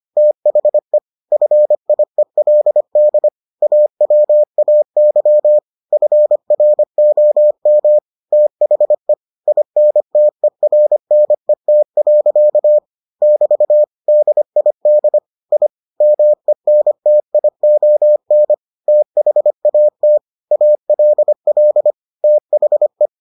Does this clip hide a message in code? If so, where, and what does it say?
Yes: Morse code from 0.2 to 23.1 s, heard as a keyed tone, about 25 wpm, 600 Hz. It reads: THE FIELD AWAY FROM THE INTERNET. <BT> DID I MENTION THAT ALL THE